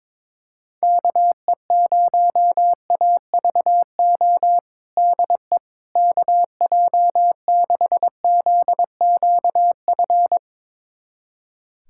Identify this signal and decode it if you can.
Morse code, heard as a keyed tone, about 21 wpm, 700 Hz. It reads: KE0AVO DE KJ6ZQF
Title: USA Callsign DE Callsign 22wpm